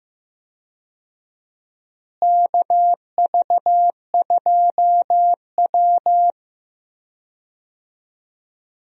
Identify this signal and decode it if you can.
Morse code, heard as a keyed tone, about 15 wpm, 700 Hz. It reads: KV2W
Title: USA Call Signs - 4 Characters - Repeated 2x - 15wpm